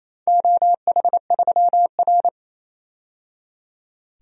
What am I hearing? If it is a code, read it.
Morse code, heard as a keyed tone, about 28 wpm, 700 Hz. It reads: OH3R
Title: DXCC-100 Call Signs - 4 Characters - 28wpm